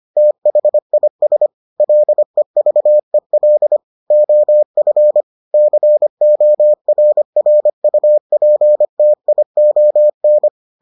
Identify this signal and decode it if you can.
Morse code, heard as a keyed tone, about 25 wpm, 600 Hz. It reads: THIS LEVEL OF CORRUPTION